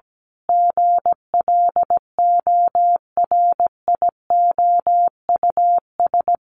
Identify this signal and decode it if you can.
Morse code, heard as a keyed tone, about 17 wpm, 700 Hz. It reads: GLORIOUS
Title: A Sound Of Thunder 17wpm